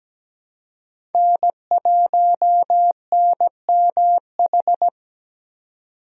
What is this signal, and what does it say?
Morse code, heard as a keyed tone, about 17 wpm, 700 Hz. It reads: N1NMH
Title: USA Call Signs - 5 Characters - 17wpm